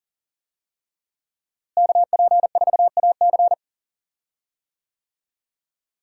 Morse code, heard as a keyed tone, about 40 wpm, 700 Hz. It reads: KP4AC
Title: USA Call Signs - 5 Characters - Repeated 3x - 40wpm